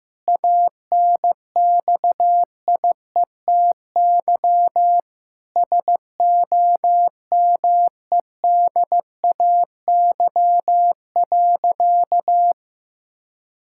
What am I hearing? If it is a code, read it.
Morse code, heard as a keyed tone, about 15 wpm, 700 Hz. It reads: ANXIETY SOMEDAY.
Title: Sentences from Top 1600 Words 15wpm